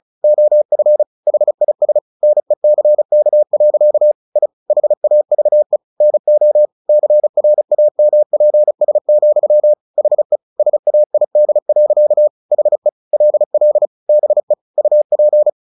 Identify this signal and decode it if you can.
Morse code, heard as a keyed tone, about 34 wpm, 600 Hz. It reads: OF HIS NECK. I HAVE NO CRAMPS, HE SAID. HE LL BE UP
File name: the old man and the sea 2 35wpm